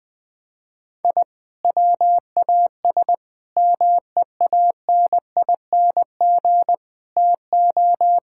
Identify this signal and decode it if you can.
Morse code, heard as a keyed tone, about 20 wpm, 700 Hz. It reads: I WAS MEANING TO
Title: Sentences from Top 1000 Words - Review Part 6 - 20wpm